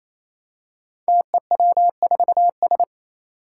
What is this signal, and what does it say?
Morse code, heard as a keyed tone, about 27 wpm, 700 Hz. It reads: TEW4S
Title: Five Challenging Letter-Number Combinations 28wpm